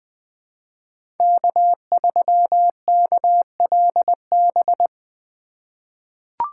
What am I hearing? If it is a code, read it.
Morse code, heard as a keyed tone, about 20 wpm, 700 Hz. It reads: K3KLB
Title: 2023 POTA Activator Call Signs - 20wpm